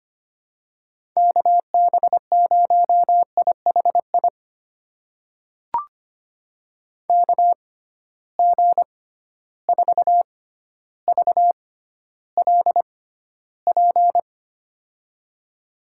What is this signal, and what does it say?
Morse code, heard as a keyed tone, about 25 wpm, spaced as for 13 wpm, 700 Hz. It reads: KB0IHI KG4VLP
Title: USA Call Signs - 6 Characters - 6x Character Spacing - 25wpm